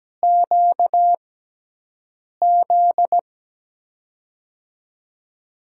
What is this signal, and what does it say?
Morse code, heard as a keyed tone, about 17 wpm, spaced as for 6 wpm, 700 Hz. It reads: QZ
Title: USA Call Signs - 6 Characters - 6x Character Spacing - 17wpm